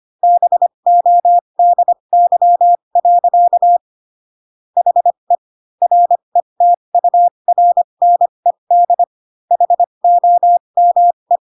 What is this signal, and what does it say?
Morse code, heard as a keyed tone, about 25 wpm, spaced as for 22 wpm, 700 Hz. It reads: BODY. HE RETURNED HOME
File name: Pinocchio / Pinocchio ch6 18 25 ik1hge